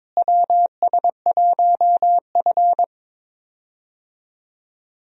Morse code, heard as a keyed tone, about 21 wpm, 700 Hz. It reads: WS1F
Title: USA Call Signs - 4 Characters - Repeated 2x - 22wpm